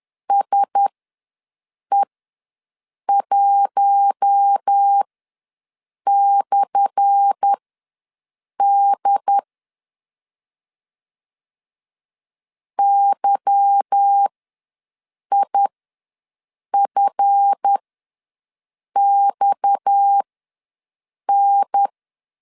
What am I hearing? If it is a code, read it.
Morse code, heard as a keyed tone, about 10 wpm, spaced as for 6 wpm, 800 Hz. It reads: SE1/D YIFXN
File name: LEZIONI / 10 SIGNS / 10A SLASH ws 10wpm 4121